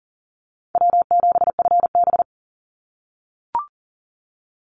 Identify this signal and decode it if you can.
Morse code, heard as a keyed tone, about 40 wpm, 700 Hz. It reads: W7FB
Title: Easy Call Signs - 40wpm